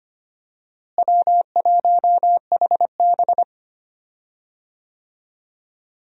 Morse code, heard as a keyed tone, about 25 wpm, 700 Hz. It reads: W1HB
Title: USA Call Signs - 4 Characters - Repeated 3x - 25wpm